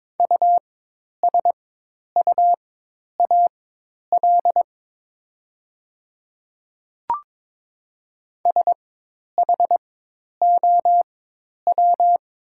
Morse code, heard as a keyed tone, about 22 wpm, spaced as for 10 wpm, 700 Hz. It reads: USUAL SHOW
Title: 4x Character Spacing - Top 500 Words - 22wpm